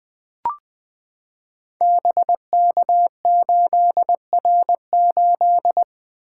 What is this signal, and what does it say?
Morse code, heard as a keyed tone, about 21 wpm, 700 Hz. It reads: BK8R8
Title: Code Groups - 5 Letter:Numbers - 20wpm